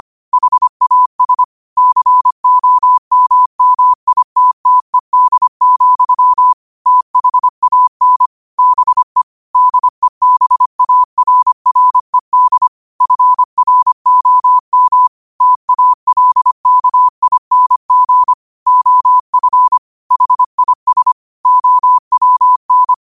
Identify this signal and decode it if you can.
Morse code, heard as a keyed tone, about 25 wpm, 1000 Hz. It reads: HAS COMMITTED, THAN BE DEBARRED FROM TALKING OF HIS OWN
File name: qotdfiles / 25wpm